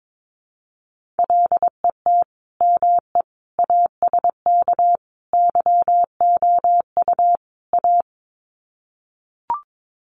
Morse code, heard as a keyed tone, about 22 wpm, 700 Hz. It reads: LET ME ASK YOU A
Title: Sets of 5 Words Encore 22wpm